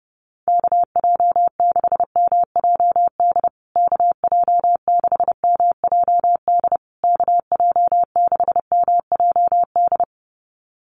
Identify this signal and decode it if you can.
Morse code, heard as a keyed tone, about 30 wpm, 700 Hz. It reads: KJ6MJD KJ6MJD KJ6MJD
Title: USA Call Signs Repeated 3x 30wpm